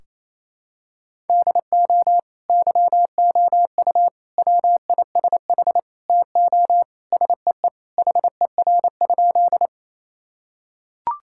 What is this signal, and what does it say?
Morse code, heard as a keyed tone, about 28 wpm, 700 Hz. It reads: DO YOU WISH TO SEE HER?
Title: Sentences from Top 1600 Words 28wpm